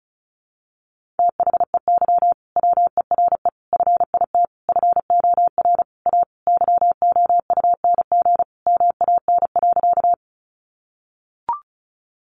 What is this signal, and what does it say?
Morse code, heard as a keyed tone, about 34 wpm, 700 Hz. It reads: THEY WERE FIT FOR A YOUNG MAN.
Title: Sentences from Top 800 Words 35wpm